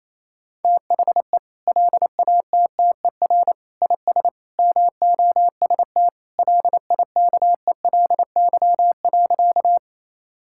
Morse code, heard as a keyed tone, about 28 wpm, 700 Hz. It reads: THE LATTER IS MOST LIKELY.
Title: Sentences from Top 1600 Words 28wpm